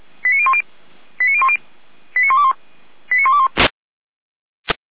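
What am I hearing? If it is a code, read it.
Five-tone selective call (ZVEI), heard as quick runs of five tones, tones of about 70 ms. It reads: 89010 89010 89121 89121
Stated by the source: ZVEI five-tone selective call received by SDR.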